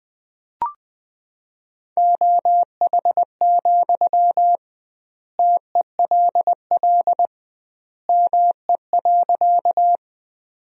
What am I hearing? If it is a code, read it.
Morse code, heard as a keyed tone, about 21 wpm, 700 Hz. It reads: OH, TELL ME.
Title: 2x Word-Spacing - Sentences from Top 500 Words - 20wpm